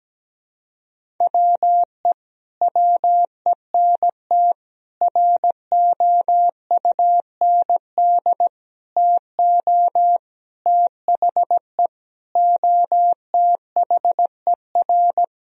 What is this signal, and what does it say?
Morse code, heard as a keyed tone, about 17 wpm, 700 Hz. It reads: WE WENT ROUND TO THE OTHER
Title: Sentences from Top 1000 Words - Review Part 1 - 17wpm